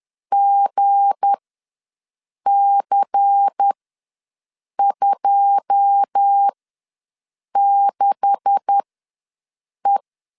Morse code, heard as a keyed tone, about 10 wpm, spaced as for 6 wpm, 800 Hz. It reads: GC26E
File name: LEZIONI / 10 SIGNS / 10A SLASH ws 10wpm 4121